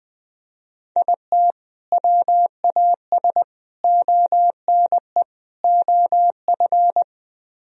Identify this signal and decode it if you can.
Morse code, heard as a keyed tone, about 21 wpm, 700 Hz. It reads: IT WAS ONE OF